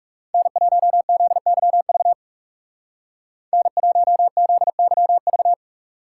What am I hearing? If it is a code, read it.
Morse code, heard as a keyed tone, about 35 wpm, 700 Hz. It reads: N1ZYV N1ZYV
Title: Speed Racing - USA Call Signs 30wpm